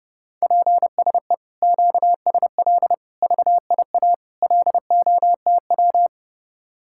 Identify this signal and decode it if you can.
Morse code, heard as a keyed tone, about 30 wpm, 700 Hz. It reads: PSE QSL VIA LOTW